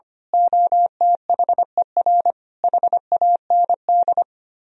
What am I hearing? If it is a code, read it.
Morse code, heard as a keyed tone, about 24 wpm, 700 Hz. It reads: OTHER HAND
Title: Speed-Racing - Sets of 5 Words 25wpm